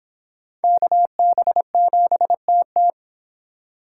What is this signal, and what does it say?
Morse code, heard as a keyed tone, about 24 wpm, 700 Hz. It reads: KB7TT